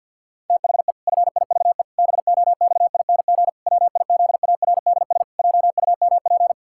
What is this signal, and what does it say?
Morse code, heard as a keyed tone, about 50 wpm, 700 Hz. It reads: THE FIVE BOXING WIZARDS JUMP
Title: Pangram Challenge v2 - 50wpm